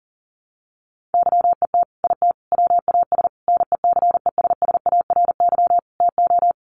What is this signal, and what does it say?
Morse code, heard as a keyed tone, about 40 wpm, 700 Hz. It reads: YET IT WAS NECESSARY TO
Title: Sentences from Top 800 Words 40wpm